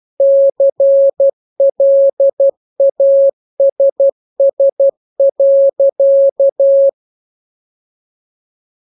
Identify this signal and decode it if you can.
Morse code, heard as a keyed tone, about 12 wpm, 550 Hz. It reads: CLASS.